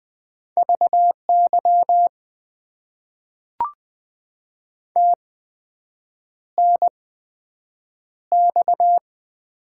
Morse code, heard as a keyed tone, about 20 wpm, spaced as for 7 wpm, 700 Hz. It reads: VY TNX
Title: Top 90 QSO Elements - 8x Character Spacing - 20wpm